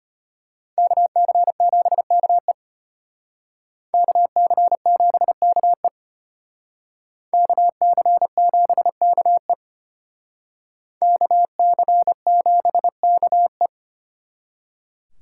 Morse code, heard as a keyed tone, about 32 wpm, 700 Hz. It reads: KC7KE KC7KE KC7KE KC7KE